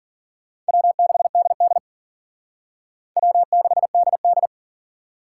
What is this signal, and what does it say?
Morse code, heard as a keyed tone, about 50 wpm, 700 Hz. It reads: W6DD W6DD
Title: Speed Racing - USA 1x2 Call Signs 40wpm